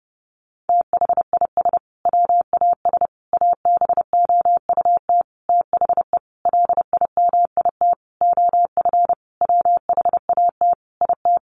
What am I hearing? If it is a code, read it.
Morse code, heard as a keyed tone, about 30 wpm, 700 Hz. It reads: THIS WAS ABOUT THE LIMIT OF WHAT IT
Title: Sentences from Top 1500 Words 30wpm